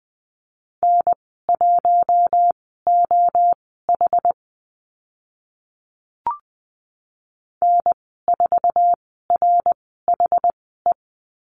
Morse code, heard as a keyed tone, about 20 wpm, spaced as for 14 wpm, 700 Hz. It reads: N1OH N4RHE
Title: 082 Lesson 14g - Number 2 - Call Signs 20wpm